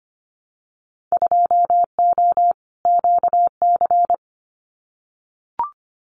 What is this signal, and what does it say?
Morse code, heard as a keyed tone, about 25 wpm, 700 Hz. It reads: 2O QC
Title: Winter Field Day Exchanges 25wpm